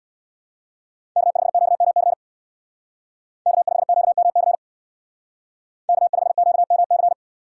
Morse code, heard as a keyed tone, about 62 wpm, 700 Hz. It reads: K5QMC K5QMC K5QMC
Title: Speed Racing - USA 1x3 Call Signs 50wpm